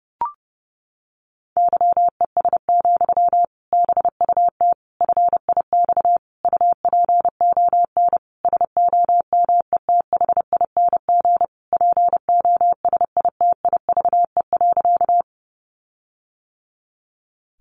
Morse code, heard as a keyed tone, about 30 wpm, 700 Hz. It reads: YES, BUT FIX UPON SOMETHING POSITIVE.
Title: Sentences from Top 1400 Words 30wpm